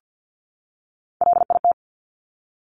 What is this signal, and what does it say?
Morse code, heard as a keyed tone, about 50 wpm, 700 Hz. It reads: LIT
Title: Top Three Letter Combinations in Words 50wpm